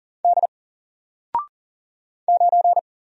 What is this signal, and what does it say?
Morse code, heard as a keyed tone, about 40 wpm, 700 Hz. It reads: D 9
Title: Single Letter-Number 40wpm